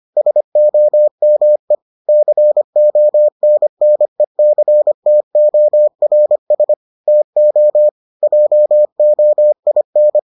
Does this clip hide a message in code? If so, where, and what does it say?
Yes: Morse code from 0.2 to 10.2 s, heard as a keyed tone, about 25 wpm, 600 Hz. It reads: SOME CONNECTORS TO JOIN